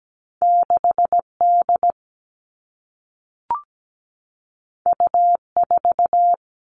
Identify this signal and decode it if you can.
Morse code, heard as a keyed tone, about 17 wpm, 700 Hz. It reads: 6D U4
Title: Two Challenging Letter-Number Combinations 17wpm